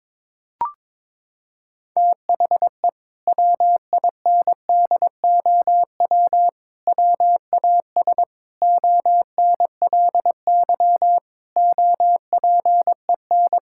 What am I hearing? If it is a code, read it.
Morse code, heard as a keyed tone, about 22 wpm, 700 Hz. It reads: THE WINDOW WAS ONLY OPEN
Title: Sentences from Top 1600 Words 22wpm